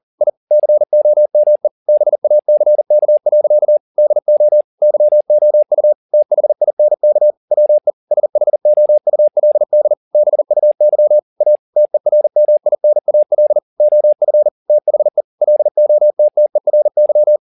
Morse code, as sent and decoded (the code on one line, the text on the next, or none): .. / -.-. --- -- . / -... .- -.-. -.- .-.-.- / -.. --- / -.-- --- ..- / - .... .. -. -.- / .-- . / ... .... --- ..- .-.. -.. / -... ..- -.-- / .- / - . .-. -- .. -. .- .-.. / --- ..-. / - .... / .-.. --- - - . .-. -.--
I COME BACK. DO YOU THINK WE SHOULD BUY A TERMINAL OF THE LOTTERY